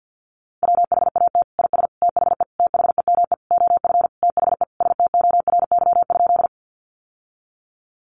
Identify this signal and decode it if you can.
Morse code, heard as a keyed tone, about 50 wpm, 700 Hz. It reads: WHAT IS THE THEME OF THE STORY?
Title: Sentences from Top 1100 Words 50wpm